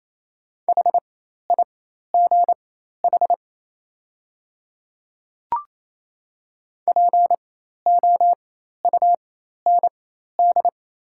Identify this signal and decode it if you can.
Morse code, heard as a keyed tone, about 28 wpm, spaced as for 13 wpm, 700 Hz. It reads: HIGH POUND
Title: 4x Character Spacing - Top 500 Words - 28wpm